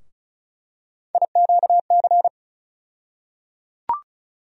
Morse code, heard as a keyed tone, about 34 wpm, 700 Hz. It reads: IQC
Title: USA Call Sign Postfixes - 3 Letters 35wpm